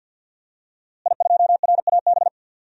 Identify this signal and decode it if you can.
Morse code, heard as a keyed tone, about 50 wpm, 700 Hz. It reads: I2RAB
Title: DXCC-100 Call Signs - 5 Characters - Repeated 3x - 50wpm